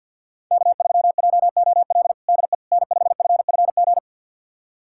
Morse code, heard as a keyed tone, about 53 wpm, 700 Hz. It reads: K3JOL DE N5FFG
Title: USA Callsign DE Callsign - 5 Characters - 50wpm